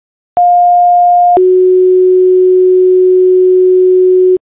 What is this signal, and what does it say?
Two-tone page, heard as a one-second tone, then a three-second tone, about 710 then 370 Hz, with noise. It is heard from 0.4 to 4.4 s.